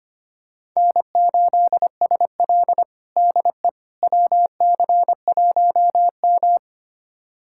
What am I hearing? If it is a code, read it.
Morse code, heard as a keyed tone, about 25 wpm, 700 Hz. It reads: N8SL DE WC1M
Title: USA Callsign DE Callsign - 4 Characters - 25wpm